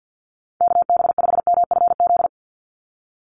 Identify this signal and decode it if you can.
Morse code, heard as a keyed tone, about 50 wpm, 700 Hz. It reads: KB5MFZ
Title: USA Call Signs 50wpm